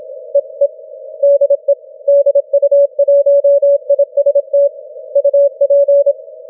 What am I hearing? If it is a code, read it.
Morse code, heard as a keyed tone, about 27 wpm, 560 Hz. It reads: EE DE DU1IST UP
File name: wp / du1ist 12m cw 31 mar 2012 0821z